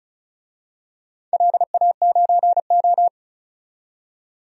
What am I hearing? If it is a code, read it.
Morse code, heard as a keyed tone, about 34 wpm, 700 Hz. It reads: LA9O
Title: DXCC-100 Call Signs - 4 Characters - 35wpm